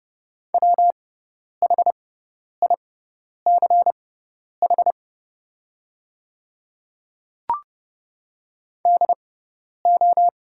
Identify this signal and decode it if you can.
Morse code, heard as a keyed tone, about 30 wpm, spaced as for 10 wpm, 700 Hz. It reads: WHICH DO
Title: 6x Character Spacing - Top 100 Words - 30wpm